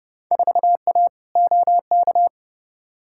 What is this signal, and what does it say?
Morse code, heard as a keyed tone, about 31 wpm, 700 Hz. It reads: 4A OK
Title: Field Day Exchanges - 30wpm